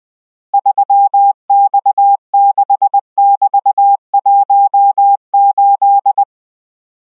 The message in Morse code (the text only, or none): 3X6<BT>18